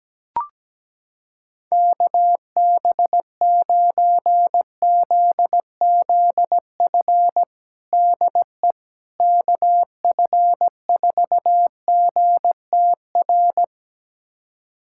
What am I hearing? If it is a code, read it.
Morse code, heard as a keyed tone, about 17 wpm, 700 Hz. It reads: KB9ZZF DE KF4GTR
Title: USA Callsign DE Callsign - 6 Characters - 17wpm